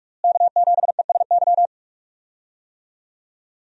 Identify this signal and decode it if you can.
Morse code, heard as a keyed tone, about 42 wpm, 700 Hz. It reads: K7ESY
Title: USA Call Signs - 5 Characters - Repeated 2x - 45wpm